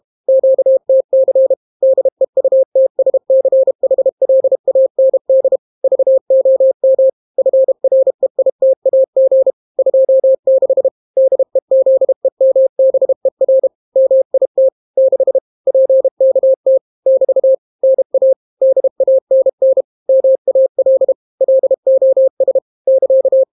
Morse code, heard as a keyed tone, about 32 wpm, 530 Hz. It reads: QTC DEUTSCHLAND VOM FREITAG 26 DEZEMBER MIT 6 PKT <BT> NA DANN MAL LOS <KA>